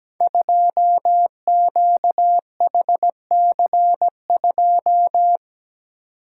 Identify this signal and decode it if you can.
Morse code, heard as a keyed tone, about 17 wpm, 700 Hz. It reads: 2QHC2
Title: Code Groups - 5 Letter:Numbers - 17wpm